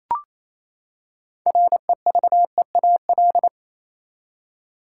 Morse code, heard as a keyed tone, about 28 wpm, 700 Hz. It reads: REVEAL
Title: Six Letter Words - 28wpm